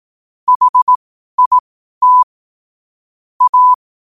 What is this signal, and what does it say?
Morse code, heard as a keyed tone, about 18 wpm, spaced as for 12 wpm, 1000 Hz. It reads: HIT A
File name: qotdfiles / 12wpm